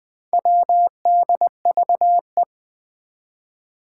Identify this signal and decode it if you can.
Morse code, heard as a keyed tone, about 20 wpm, 700 Hz. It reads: WDVE
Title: Four Challenging Letter-Number Combinations 20wpm